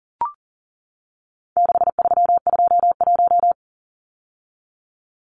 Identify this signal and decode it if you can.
Morse code, heard as a keyed tone, about 40 wpm, 700 Hz. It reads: 6321
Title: Numbers - 4-Digit 40wpm